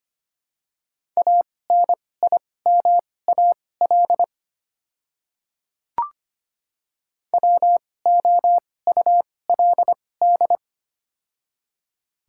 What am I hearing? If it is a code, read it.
Morse code, heard as a keyed tone, about 25 wpm, spaced as for 18 wpm, 700 Hz. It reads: ANIMAL WOULD
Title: 2x Character Spacing - Top 500 Words - 25wpm